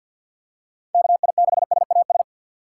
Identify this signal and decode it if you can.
Morse code, heard as a keyed tone, about 52 wpm, 700 Hz. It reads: KI6SAS
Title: USA Call Signs - 6 Characters - 4x Character Spacing - 50wpm